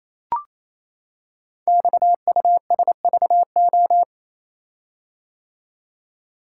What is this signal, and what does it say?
Morse code, heard as a keyed tone, about 28 wpm, 700 Hz. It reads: XUSVO
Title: Code Groups - 5 Letter:Numbers - 28wpm